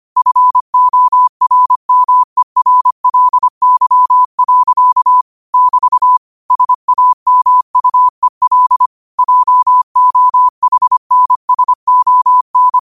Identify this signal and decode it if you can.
Morse code, heard as a keyed tone, about 25 wpm, 1000 Hz. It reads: FORMERLY. <BT> SAMUEL JOHNSON